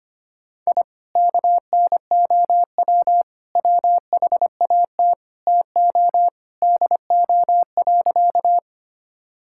Morse code, heard as a keyed tone, about 25 wpm, 700 Hz. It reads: I KNOW WHAT TO DO.
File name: practice / 25wpm / Sentences with 3-Word Phrases - 25wpm